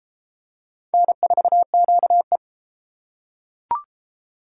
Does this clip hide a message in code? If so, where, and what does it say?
Yes: Morse code from 0.9 to 2.4 s, heard as a keyed tone, about 30 wpm, 700 Hz. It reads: N4QE